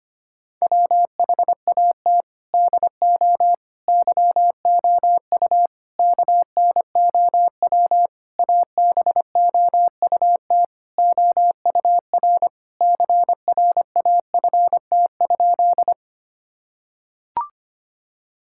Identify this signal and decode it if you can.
Morse code, heard as a keyed tone, about 25 wpm, 700 Hz. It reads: WHAT DO YOU KNOW ABOUT OUR CRAFT?
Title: Sentences from Top 1000 Words - Review Part 6 - 25wpm